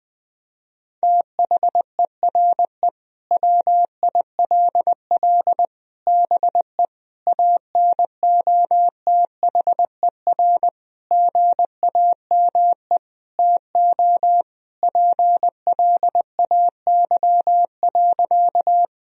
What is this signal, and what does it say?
Morse code, heard as a keyed tone, about 20 wpm, 700 Hz. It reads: THERE WILL BE ANOTHER GAME TO PLAY.
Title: Sentences from Top 1000 Words - Review Part 3 - 20wpm